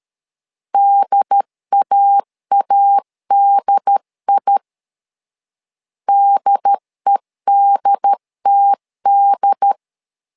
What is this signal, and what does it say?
Morse code, heard as a keyed tone, about 13 wpm, 800 Hz. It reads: DAADI DEDTD
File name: LEZIONI / 2 UHDN / 2 UHD 12wpm 1540